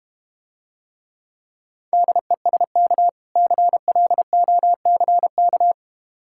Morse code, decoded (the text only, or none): DESK CLOCK